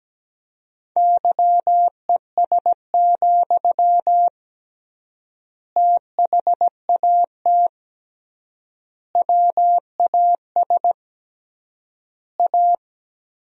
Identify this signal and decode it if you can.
Morse code, heard as a keyed tone, about 17 wpm, 700 Hz. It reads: YES, THAT WAS A
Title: 3x Word-Spacing - Sentences from Top 500 Words - 17wpm